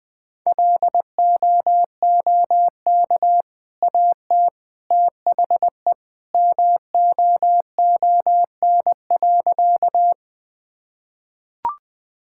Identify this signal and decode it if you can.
Morse code, heard as a keyed tone, about 21 wpm, 700 Hz. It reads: LOOK AT THE MOON.
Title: Easy Sentences with Hints - 20wpm